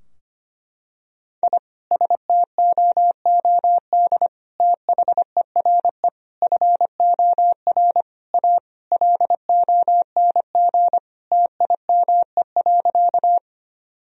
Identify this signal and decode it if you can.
Morse code, heard as a keyed tone, about 25 wpm, 700 Hz. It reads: I STOOD THERE FOR A LONG TIME.
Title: Sentences from Top 1000 Words - Review Part 2 - 25wpm